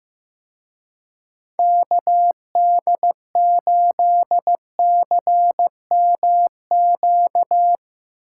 Morse code, heard as a keyed tone, about 16 wpm, 700 Hz. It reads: KD8CMQ
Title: Speed Racing - USA 2x3 Call Signs 15wpm